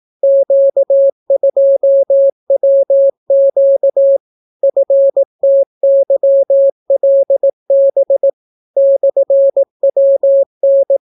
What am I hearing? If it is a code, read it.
Morse code, heard as a keyed tone, about 18 wpm, 550 Hz. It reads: Q2WQ FTYLB /WN